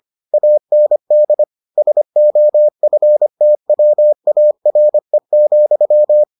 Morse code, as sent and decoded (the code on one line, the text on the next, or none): .- -. -.. / ... --- ..-. - .-- .- .-. --..--
AND SOFTWARE,